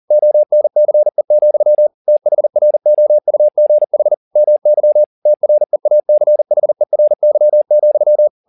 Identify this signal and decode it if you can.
Morse code, heard as a keyed tone, about 40 wpm, 600 Hz. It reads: ONCE, THROUGH MY TREACHERY,